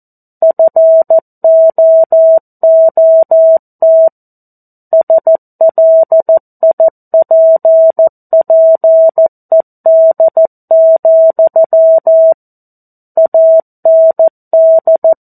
Morse code, heard as a keyed tone, about 14 wpm, 650 Hz. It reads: FOOT SLIPPED, AND
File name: 02-Chapter-02 14 WPM